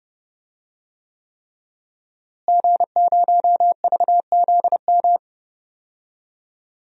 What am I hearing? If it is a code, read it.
Morse code, heard as a keyed tone, about 30 wpm, 700 Hz. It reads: G0VZM
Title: DXCC-100 Call Signs - 5 Characters - Repeated 3x - 30wpm